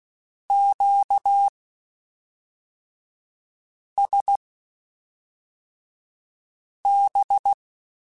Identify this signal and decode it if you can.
Morse code, heard as a keyed tone, about 16 wpm, spaced as for 3 wpm, 800 Hz. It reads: QSB
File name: (spaced) / Q-Calls Spaced-mp3s / Q Calls sp Once @5